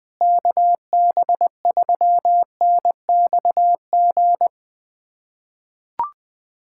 Morse code, heard as a keyed tone, about 20 wpm, 700 Hz. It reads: KB3NXG
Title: USA Callsign DE Callsign - 6 Characters - 20wpm